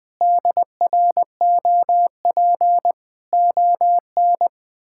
Morse code, heard as a keyed tone, about 21 wpm, 700 Hz. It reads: DROP ON
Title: Pangram Challenge v2 - 20wpm